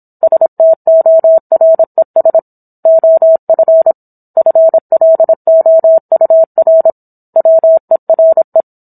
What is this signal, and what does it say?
Morse code, heard as a keyed tone, about 27 wpm, 650 Hz. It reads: STORES OF FLOUR WERE